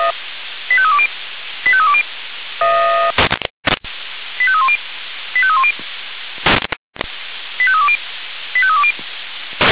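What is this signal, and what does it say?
Five-tone selective call (ZVEI), heard as quick runs of five tones, tones of about 70 ms. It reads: 85320 85320 85310 85310 85320 85320
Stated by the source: ZVEI five-tone selective call received on SDR.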